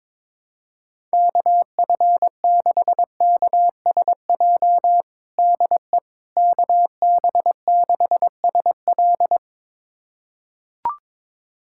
Morse code, heard as a keyed tone, about 22 wpm, 700 Hz. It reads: KF6KSJ DE KB6SL
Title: USA Callsign DE Callsign 22wpm